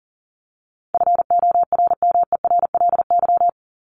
Morse code, heard as a keyed tone, about 40 wpm, 700 Hz. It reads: FORMERLY
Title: LY Suffix - Top 100 Words with Base Words 40wpm